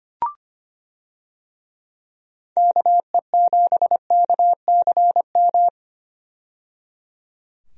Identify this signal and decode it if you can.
Morse code, heard as a keyed tone, about 25 wpm, 700 Hz. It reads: KE7KCM
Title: USA Call Signs - 6 Characters - 25wpm